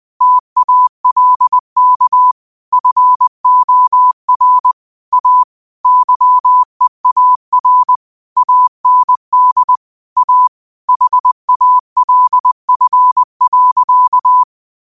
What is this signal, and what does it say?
Morse code, heard as a keyed tone, about 20 wpm, 1000 Hz. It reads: TALK FOR A YEAR AND A HALF.